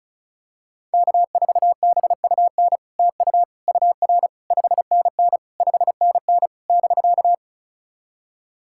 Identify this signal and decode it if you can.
Morse code, heard as a keyed tone, about 36 wpm, 700 Hz. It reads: K4BUN TU UR 5NN 5NN <BK>
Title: SOTA QSO Practice 35wpm